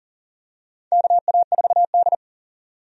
Morse code, heard as a keyed tone, about 40 wpm, 700 Hz. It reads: KA4D
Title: USA Call Signs - 4 Characters - 40wpm